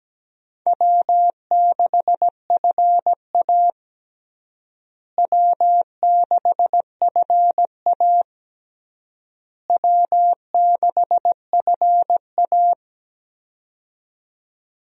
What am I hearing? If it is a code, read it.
Morse code, heard as a keyed tone, about 17 wpm, 700 Hz. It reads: W6FA W6FA W6FA
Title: USA Call Signs - 4 Characters - Repeated 3x - 17wpm